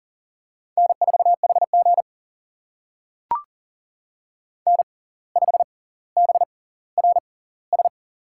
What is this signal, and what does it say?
Morse code, heard as a keyed tone, about 40 wpm, spaced as for 19 wpm, 700 Hz. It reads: N4HG N5BRS
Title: USA Call Signs - 6x Character Spacing - 40wpm